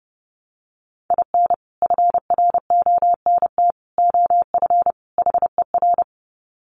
Morse code, heard as a keyed tone, about 30 wpm, 700 Hz. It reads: IN FRONT OF HER
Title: Sets of 4 Words 30wpm